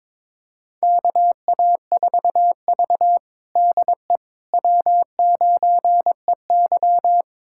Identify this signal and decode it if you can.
Morse code, heard as a keyed tone, about 23 wpm, 700 Hz. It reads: KA4V DE W9EY
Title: USA Callsign DE Callsign - 4 Characters - 22wpm